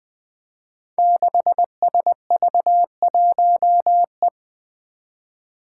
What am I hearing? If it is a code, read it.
Morse code, heard as a keyed tone, about 20 wpm, 700 Hz. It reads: 6SV1E
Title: Five Challenging Letter-Number Combinations 20wpm